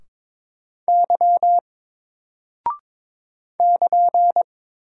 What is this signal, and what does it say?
Morse code, heard as a keyed tone, about 22 wpm, 700 Hz. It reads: Y <KN>
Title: Single Character (Letter:Number:Punctuation:Prosign) 22wpm